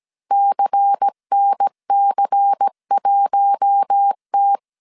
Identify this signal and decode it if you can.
Morse code, heard as a keyed tone, about 17 wpm, 800 Hz. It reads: CNC1T
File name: CW / DET / 1 6 1500C S A 16wpm 2635